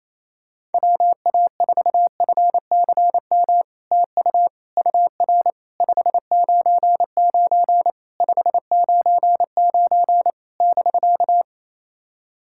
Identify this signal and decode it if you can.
Morse code, heard as a keyed tone, about 27 wpm, 700 Hz. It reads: WA4FCM TU UR 599 599 <BK>
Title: SOTA QSO Practice 28wpm